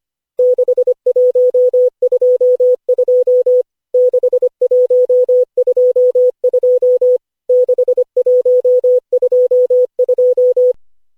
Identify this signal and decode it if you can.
Morse code, heard as a keyed tone, about 25 wpm, 500 Hz. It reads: 6122 6122 6122
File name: cw / Session5 cw training